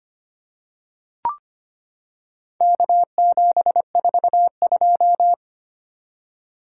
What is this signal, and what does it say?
Morse code, heard as a keyed tone, about 25 wpm, 700 Hz. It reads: K742
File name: practice / 25wpm / Four Letter-Number Combinations - 25wpm